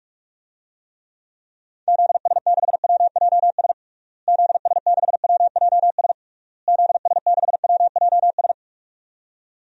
Morse code, heard as a keyed tone, about 46 wpm, 700 Hz. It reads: ZS6WJS ZS6WJS ZS6WJS
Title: DXCC-100 Call Signs - 6 Characters - Repeated 3x - 45wpm